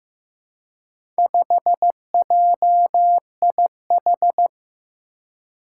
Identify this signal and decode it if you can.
Morse code, heard as a keyed tone, about 15 wpm, 700 Hz. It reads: 5JIH
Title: Four Challenging Letter-Number Combinations 15wpm